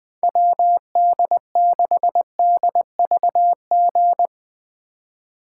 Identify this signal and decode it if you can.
Morse code, heard as a keyed tone, about 21 wpm, 700 Hz. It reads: WD6DVG
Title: USA Call Signs - 6 Characters - 2x Character Spacing - 20wpm